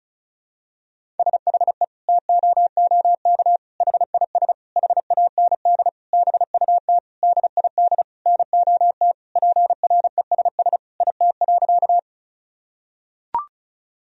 Morse code, heard as a keyed tone, about 36 wpm, 700 Hz. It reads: SHE TOOK HIS HAND BUT DID NOT PRESS IT.
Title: Sentences from Top 1000 Words - Review Part 1 - 35wpm